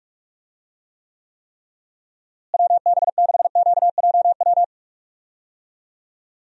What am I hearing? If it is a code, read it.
Morse code, heard as a keyed tone, about 47 wpm, 700 Hz. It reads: WB6QJW
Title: USA Call Signs - 6 Characters - Repeated 3x - 45wpm